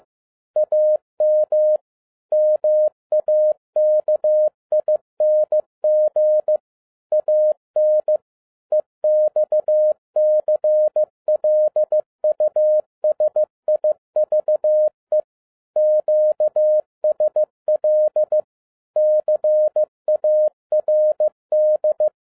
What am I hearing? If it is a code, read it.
Morse code, heard as a keyed tone, about 15 wpm, 610 Hz. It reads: AM MAKING AN EXCLUSIVE QSL CARD
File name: 2011-10-mp3 / n02-FIELD DAY CONTEST QRV FROM SAKAE VILLAGE 15wpm